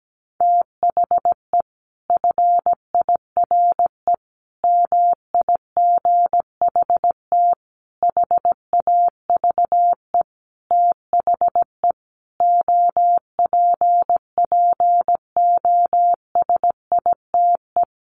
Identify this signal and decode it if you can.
Morse code, heard as a keyed tone, about 17 wpm, 700 Hz. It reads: THE FIRE MIGHT HAVE THE OPPOSITE
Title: Sentences from Top 1300 Words 17wpm